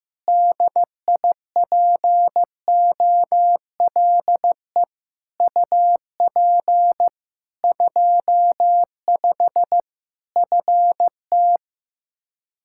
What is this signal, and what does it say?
Morse code, heard as a keyed tone, about 15 wpm, 700 Hz. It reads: DIPOLE UP 25 FT
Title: Practice QSO - Exchange 2 - Info - 15wpm